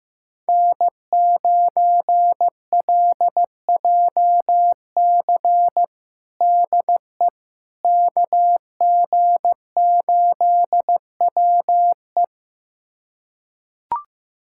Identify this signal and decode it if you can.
Morse code, heard as a keyed tone, about 15 wpm, 700 Hz. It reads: N9LJC DE KG8WE
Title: USA Callsign DE Callsign - 5 Characters - 15wpm